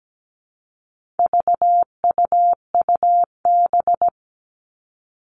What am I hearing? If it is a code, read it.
Morse code, heard as a keyed tone, about 17 wpm, 700 Hz. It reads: VUUB
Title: All Cut Numbers - 4-Digit 17wpm